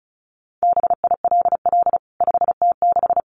Morse code, heard as a keyed tone, about 34 wpm, 700 Hz. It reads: BILL 5T6
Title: CWT Exchange 35wpm